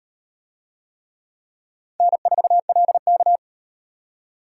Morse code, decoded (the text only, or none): N4LK